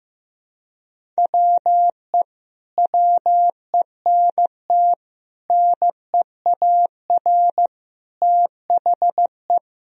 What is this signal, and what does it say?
Morse code, heard as a keyed tone, about 15 wpm, 700 Hz. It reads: WE WENT NEAR THE
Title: Sentences from Top 1000 Words - Review Part 1 - 15wpm